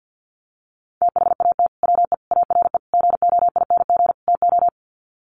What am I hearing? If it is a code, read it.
Morse code, heard as a keyed tone, about 50 wpm, 700 Hz. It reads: THAT WE ARE GOING TO
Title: Sets of 5 Words Encore 50wpm